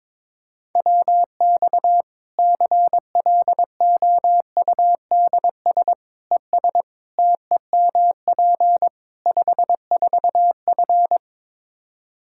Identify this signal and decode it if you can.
Morse code, heard as a keyed tone, about 22 wpm, 700 Hz. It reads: WX CLOUDS ES TEMP 54F
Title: Practice QSO - Exchange 2 - Info - 22wpm